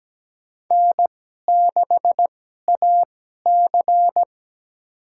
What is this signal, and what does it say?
Morse code, heard as a keyed tone, about 17 wpm, spaced as for 12 wpm, 700 Hz. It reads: N6AC
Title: USA Call Signs - 4 Characters - 2x Character Spacing - 17wpm